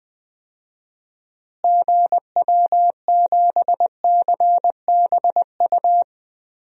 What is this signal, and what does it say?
Morse code, heard as a keyed tone, about 20 wpm, 700 Hz. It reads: GW7CBU
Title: DXCC-100 Call Signs - 6 Characters - Repeated 2x - 20wpm